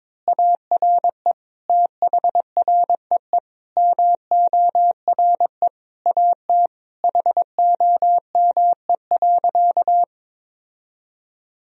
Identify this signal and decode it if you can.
Morse code, heard as a keyed tone, about 22 wpm, 700 Hz. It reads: ARE THREE MORE AT HOME.
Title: Sentences from Top 1000 Words - Review Part 1 - 22wpm